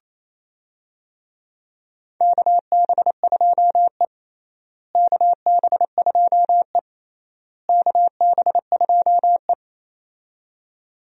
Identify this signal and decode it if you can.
Morse code, heard as a keyed tone, about 28 wpm, 700 Hz. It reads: KB2E KB2E KB2E
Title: USA Call Signs - 4 Characters - Repeated 3x - 28wpm